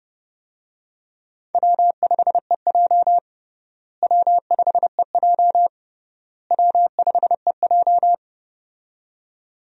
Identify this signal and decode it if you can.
Morse code, heard as a keyed tone, about 30 wpm, 700 Hz. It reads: W5EJ W5EJ W5EJ
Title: USA Call Signs - 4 Characters - Repeated 3x - 30wpm